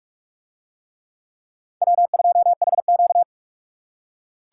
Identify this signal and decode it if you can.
Morse code, heard as a keyed tone, about 41 wpm, 700 Hz. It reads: W2HQ